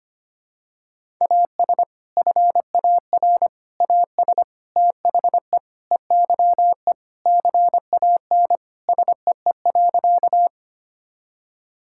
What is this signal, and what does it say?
Morse code, heard as a keyed tone, about 25 wpm, 700 Hz. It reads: AS FAR AS THE EYE CAN SEE.
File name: practice / 25wpm / Sentences with 4-Word Phrases - 25wpm